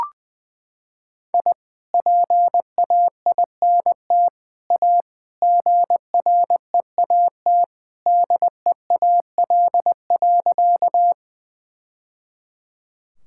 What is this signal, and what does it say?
Morse code, heard as a keyed tone, about 21 wpm, 700 Hz. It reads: I PAINT A GREAT DEAL.
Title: Sentences from Top 1000 Words - Review Part 6 - 20wpm